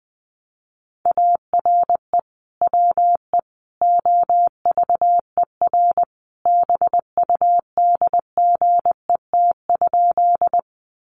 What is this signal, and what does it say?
Morse code, heard as a keyed tone, about 20 wpm, 700 Hz. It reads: ARE WE OVER BUDGET?
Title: Sentences from Top 1200 Words 20wpm